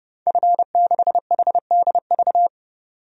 Morse code, heard as a keyed tone, about 30 wpm, 700 Hz. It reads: F6HDV
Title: DXCC-100 Call Signs - 5 Characters - 30wpm